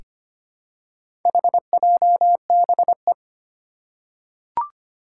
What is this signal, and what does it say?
Morse code, heard as a keyed tone, about 25 wpm, 700 Hz. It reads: HJBE
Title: Four Challenging Letter-Number Combinations 25wpm